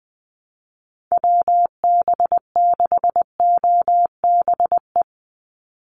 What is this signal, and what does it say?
Morse code, heard as a keyed tone, about 20 wpm, 700 Hz. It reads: WB6OBE
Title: USA Call Signs 20wpm